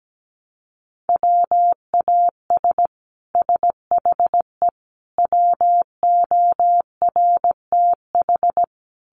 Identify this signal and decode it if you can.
Morse code, heard as a keyed tone, about 17 wpm, 700 Hz. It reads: WAS SHE WORTH
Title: Sentences from Top 1200 Words 17wpm